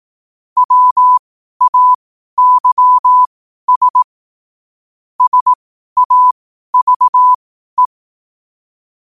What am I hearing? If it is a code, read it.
Morse code, heard as a keyed tone, about 18 wpm, spaced as for 12 wpm, 1000 Hz. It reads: WAYS SAVE
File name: qotdfiles / 12wpm